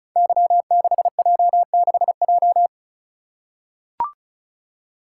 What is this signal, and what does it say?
Morse code, heard as a keyed tone, about 36 wpm, 700 Hz. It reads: Y6J6J
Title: Code Groups - 5 Letter:Numbers - 35wpm